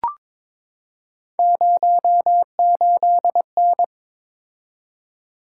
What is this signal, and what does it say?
Morse code, heard as a keyed tone, about 22 wpm, 700 Hz. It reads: 08N
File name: practice / 22wpm / Three Letter-Number Combinations - 22wpm